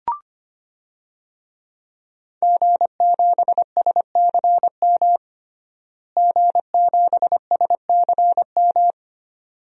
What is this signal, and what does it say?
Morse code, heard as a keyed tone, about 25 wpm, 700 Hz. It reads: G7SCM G7SCM
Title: DXCC-100 Call Signs - 5 Characters - Repeated 2x - 25wpm